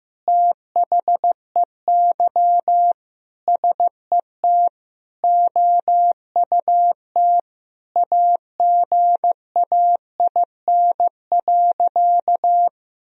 Morse code, heard as a keyed tone, about 15 wpm, 700 Hz. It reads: THEY SET OUT AGAIN.
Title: Sentences from Top 1000 Words - Review Part 4 - 15wpm